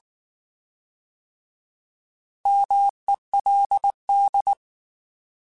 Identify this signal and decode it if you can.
Morse code, heard as a keyed tone, about 19 wpm, 800 Hz. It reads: MELD